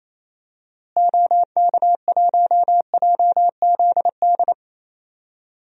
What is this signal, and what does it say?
Morse code, heard as a keyed tone, about 28 wpm, 700 Hz. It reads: OK1JZD
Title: DXCC-100 Call Signs - 6 Characters - Repeated 2x - 28wpm